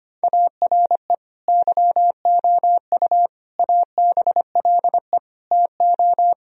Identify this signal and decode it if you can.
Morse code, heard as a keyed tone, about 25 wpm, 700 Hz. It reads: ARE YOU ABLE TO
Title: Sentences from Top 1000 Words - Review Part 3 - 25wpm